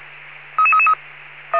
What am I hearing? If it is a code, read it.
Five-tone selective call (ZVEI), heard as quick runs of five tones, tones of about 70 ms. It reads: 33333
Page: SDR/ZVEI